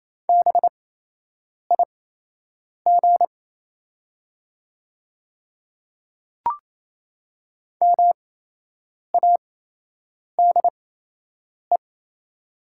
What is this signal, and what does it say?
Morse code, heard as a keyed tone, about 28 wpm, spaced as for 8 wpm, 700 Hz. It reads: BIG MADE
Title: 8x Character Spacing - Top 200 Words - 28wpm